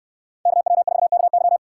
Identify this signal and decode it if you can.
Morse code, heard as a keyed tone, about 68 wpm, 700 Hz. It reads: KW4KY